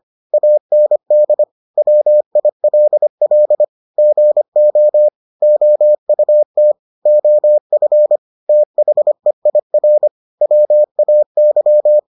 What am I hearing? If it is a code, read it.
Morse code, heard as a keyed tone, about 25 wpm, 600 Hz. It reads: AND WILL GO OUT OF THEIR WAY